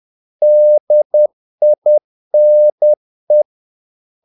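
Morse code, heard as a keyed tone, about 10 wpm, 600 Hz. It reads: DINE
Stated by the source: Morse practice of short Danish words at 10 wpm, from CW med Gnister.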